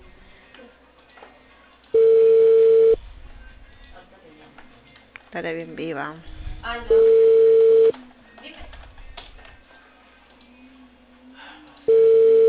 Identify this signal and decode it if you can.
Ringback tone (China, 450 Hz), with speech.